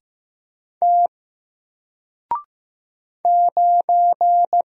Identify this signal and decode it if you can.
Morse code, heard as a keyed tone, about 15 wpm, 700 Hz. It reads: T 9
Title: Single Letter-Number Alt 15wpm